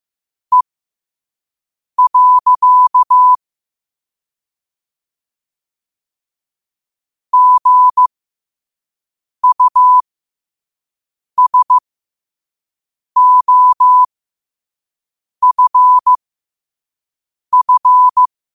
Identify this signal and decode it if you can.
Morse code, heard as a keyed tone, about 15 wpm, spaced as for 5 wpm, 1000 Hz. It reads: E. GUSOFF
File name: qotdfiles / 5wpm